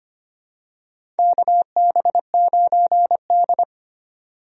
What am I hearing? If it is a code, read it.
Morse code, heard as a keyed tone, about 25 wpm, 700 Hz. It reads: KB9D
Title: USA Call Signs - 4 Characters - 25wpm